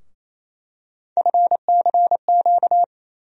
Morse code, heard as a keyed tone, about 28 wpm, 700 Hz. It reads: FCQ
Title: USA Call Sign Postfixes - 3 Letters 28wpm